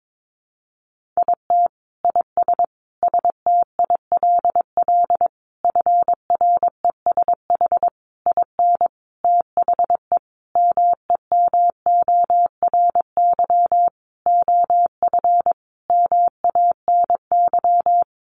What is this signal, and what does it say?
Morse code, heard as a keyed tone, about 22 wpm, 700 Hz. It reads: IT IS STILL FRESH IN THE MEMORY OF MANY
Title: Sentences from Top 1300 Words 22wpm